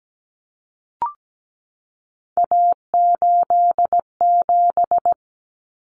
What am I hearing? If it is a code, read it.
Morse code, heard as a keyed tone, about 17 wpm, 700 Hz. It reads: A87
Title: Common Cut Numbers - 3-Digit 17wpm